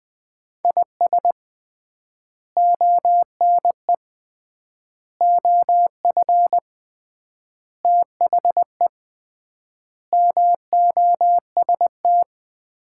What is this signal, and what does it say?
Morse code, heard as a keyed tone, about 20 wpm, 700 Hz. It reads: IS ONE OF THE MOST
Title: 3x Word-Spacing - Sets of 5 Words - 20wpm